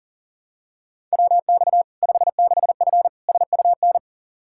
Morse code, heard as a keyed tone, about 40 wpm, 700 Hz. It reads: WX 56F SUN
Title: Practice QSO - Exchange 2 - Info - 40wpm